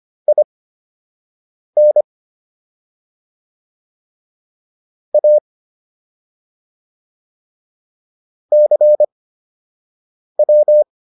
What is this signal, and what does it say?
Morse code, heard as a keyed tone, about 25 wpm, spaced as for 6 wpm, 600 Hz. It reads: IN A CW